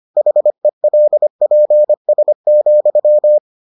Morse code, heard as a keyed tone, about 25 wpm, 600 Hz. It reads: HELPS,